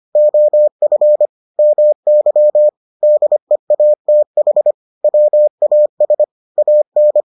Morse code, heard as a keyed tone, about 25 wpm, 600 Hz. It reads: OF MY DEATH WAS AN